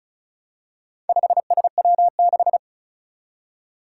Morse code, heard as a keyed tone, about 36 wpm, 700 Hz. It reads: 5SW6
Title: Four Challenging Letter-Number Combinations 35wpm